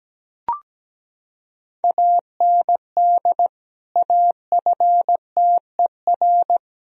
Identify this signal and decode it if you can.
Morse code, heard as a keyed tone, about 17 wpm, 700 Hz. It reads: AND AFTER
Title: Sentences from Top 1000 Words - Review Part 2 - 17wpm